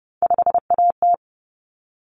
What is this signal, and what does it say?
Morse code, heard as a keyed tone, about 30 wpm, 700 Hz. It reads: 5AT
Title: Common Cut Numbers - 3-Digit 30wpm